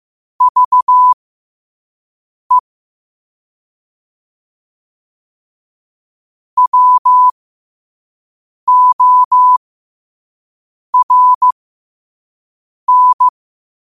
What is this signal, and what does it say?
Morse code, heard as a keyed tone, about 15 wpm, spaced as for 5 wpm, 1000 Hz. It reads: VE WORN
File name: qotdfiles / 5wpm